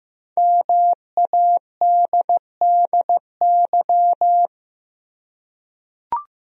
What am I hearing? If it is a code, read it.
Morse code, heard as a keyed tone, about 15 wpm, 700 Hz. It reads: MADDY
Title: USA Common Names - Max 5 Letters - 15wpm